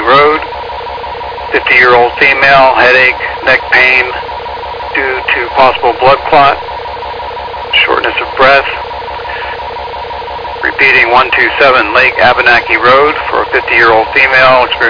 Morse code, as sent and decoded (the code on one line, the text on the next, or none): none